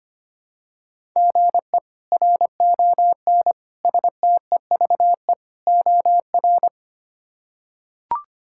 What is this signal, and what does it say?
Morse code, heard as a keyed tone, about 25 wpm, 700 Hz. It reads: GE RON STEVE OR
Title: K1USN SST QSO Practice - 25wpm